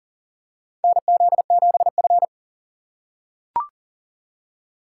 Morse code, heard as a keyed tone, about 39 wpm, 700 Hz. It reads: NZ7F
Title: USA Call Signs - 4 Characters - 4x Character Spacing - 40wpm